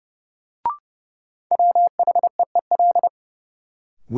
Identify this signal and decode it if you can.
Morse code, heard as a keyed tone, about 30 wpm, 700 Hz. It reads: WHEEL
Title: Sentences from Top 500 Words - New Word Drill - 30wpm